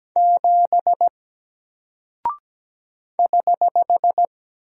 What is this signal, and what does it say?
Morse code, heard as a keyed tone, about 17 wpm, 700 Hz. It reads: 7 <HH>
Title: Single Character (Letter:Number:Punctuation:Prosign) Alt 17wpm